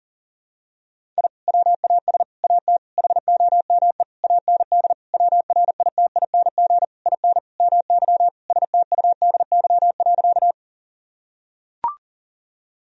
Morse code, heard as a keyed tone, about 40 wpm, 700 Hz. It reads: I WAS AT HOME AND WRITING IN MY STUDY.
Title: Sentences from Top 1000 Words - Review Part 1 - 40wpm